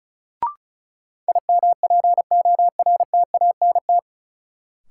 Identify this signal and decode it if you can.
Morse code, heard as a keyed tone, about 36 wpm, 700 Hz. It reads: IMPORTANT
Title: Sets of 3 Words - New Word Drill - 35wpm